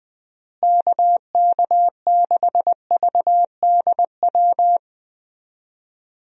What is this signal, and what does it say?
Morse code, heard as a keyed tone, about 20 wpm, 700 Hz. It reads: KK6VDW
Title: USA Call Signs - 6 Characters - Repeated 3x - 20wpm